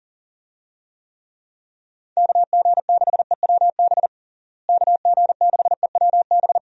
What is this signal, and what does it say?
Morse code, heard as a keyed tone, about 40 wpm, 700 Hz. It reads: KG6EWB KG6EWB
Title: USA Call Signs - 6 Characters - Repeated 3x - 40wpm